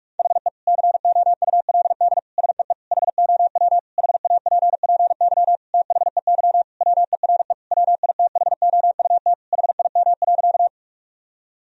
Morse code, heard as a keyed tone, about 46 wpm, 700 Hz. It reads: HE COULD SEE HOW HAPPY THEY WERE WITHOUT HIM.
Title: Sentences from Top 1000 Words - Review Part 3 - 45wpm